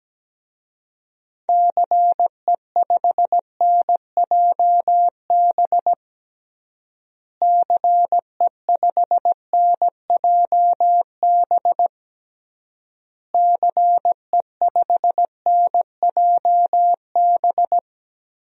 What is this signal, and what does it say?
Morse code, heard as a keyed tone, about 17 wpm, 700 Hz. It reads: CE5NJB CE5NJB CE5NJB
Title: DXCC-100 Call Signs - 6 Characters - Repeated 3x - 17wpm